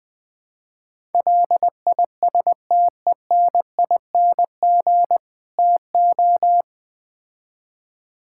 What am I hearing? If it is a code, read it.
Morse code, heard as a keyed tone, about 20 wpm, 700 Hz. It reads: LISTENING TO
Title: Build A Phrase - 5-Word Phrases - 20wpm